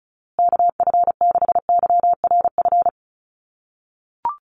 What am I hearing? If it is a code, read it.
Morse code, heard as a keyed tone, about 34 wpm, 700 Hz. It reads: KF6YRF
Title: USA Call Signs Repeated 2x 35wpm